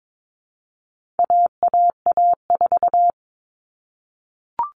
Morse code, heard as a keyed tone, about 22 wpm, 700 Hz. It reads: AAA4
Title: All Cut Numbers - 4-Digit 22wpm